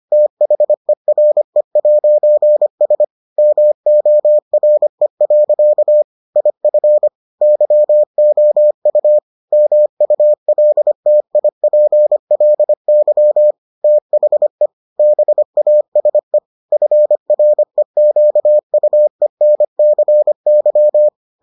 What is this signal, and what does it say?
Morse code, heard as a keyed tone, about 25 wpm, 600 Hz. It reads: THERE'S MORE. IF YOU MULTIPLY THE BASE FREQUENCY